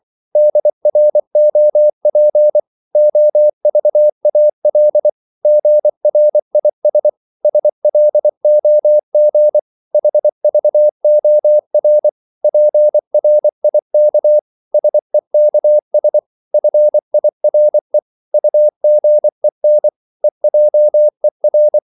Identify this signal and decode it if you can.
Morse code, heard as a keyed tone, about 24 wpm, 600 Hz. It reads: DROP OVAL GRIS SLOG HVOR PRIK SEKS FIRE UGEN EJER
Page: Korte danske ord 24wpm